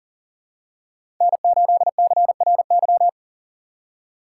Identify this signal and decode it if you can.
Morse code, heard as a keyed tone, about 40 wpm, 700 Hz. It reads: N8CRY
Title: USA Call Signs - 5 Characters - 2x Character Spacing - 40wpm